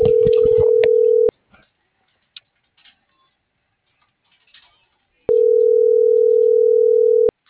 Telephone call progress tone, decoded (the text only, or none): ringback tone